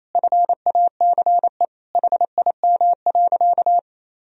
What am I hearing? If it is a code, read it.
Morse code, heard as a keyed tone, about 28 wpm, 700 Hz. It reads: FACE HIM.
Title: Sentences from Top 1000 Words - Review Part 2 - 28wpm